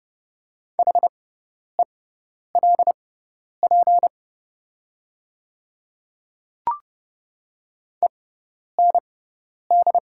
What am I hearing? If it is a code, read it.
Morse code, heard as a keyed tone, about 29 wpm, spaced as for 10 wpm, 690 Hz. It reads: HELP END
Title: 6x Character Spacing - Top 200 Words - 30wpm